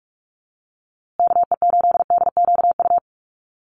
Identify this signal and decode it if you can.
Morse code, heard as a keyed tone, about 44 wpm, 700 Hz. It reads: KE8DQU
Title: USA Call Signs Repeated 3x 45wpm